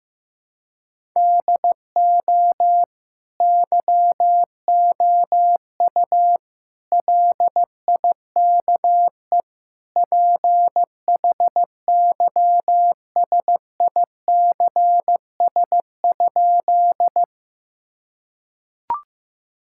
Morse code, heard as a keyed tone, about 15 wpm, 700 Hz. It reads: DO YOU LIKE PHYSICS?
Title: Sentences from Top 1000 Words - Review Part 5 - 15wpm